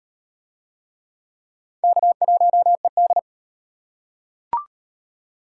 Morse code, heard as a keyed tone, about 38 wpm, 700 Hz. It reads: K1ED